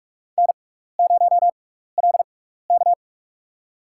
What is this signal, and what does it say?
Morse code, heard as a keyed tone, about 46 wpm, spaced as for 16 wpm, 700 Hz. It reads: N0LK
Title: USA Call Signs - 4 Characters - 6x Character Spacing - 45wpm